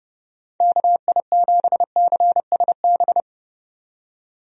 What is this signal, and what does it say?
Morse code, heard as a keyed tone, about 30 wpm, 700 Hz. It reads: KI7CSB
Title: USA Call Signs - 6 Characters - 30wpm